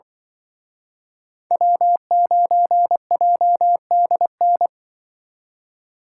Morse code, heard as a keyed tone, about 24 wpm, 700 Hz. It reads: W9JDN
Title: Speed Racing - USA 1x3 Call Signs 20wpm